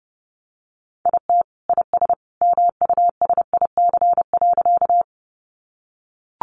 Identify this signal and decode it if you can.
Morse code, heard as a keyed tone, about 30 wpm, 700 Hz. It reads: IT IS MUSIC.
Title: Sentences from Top 300 Words 30wpm